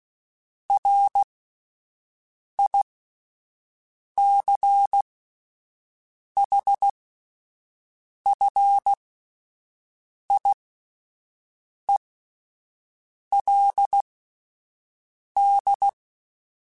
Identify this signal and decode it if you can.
Morse code, heard as a keyed tone, about 16 wpm, spaced as for 5 wpm, 800 Hz. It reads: RICHFIELD